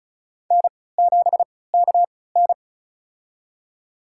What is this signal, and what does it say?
Morse code, heard as a keyed tone, about 36 wpm, spaced as for 20 wpm, 700 Hz. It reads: N7KN
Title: USA Call Signs - 3x Character Spacing - 35wpm